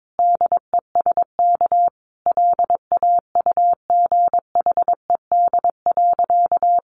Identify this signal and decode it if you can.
Morse code, heard as a keyed tone, about 22 wpm, 700 Hz. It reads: DESK LAUGHED.
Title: A Sound Of Thunder 22wpm